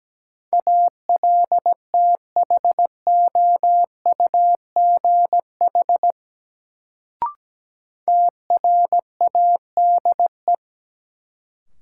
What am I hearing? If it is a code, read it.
Morse code, heard as a keyed tone, about 17 wpm, 700 Hz. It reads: ALTHOUGH TRADE
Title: Sentences from Top 600 Words - New Word Drill - 17wpm